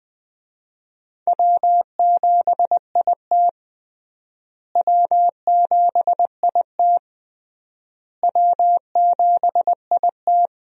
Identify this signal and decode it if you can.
Morse code, heard as a keyed tone, about 21 wpm, 700 Hz. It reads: W7IT W7IT W7IT
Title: USA Call Signs - 4 Characters - Repeated 3x - 20wpm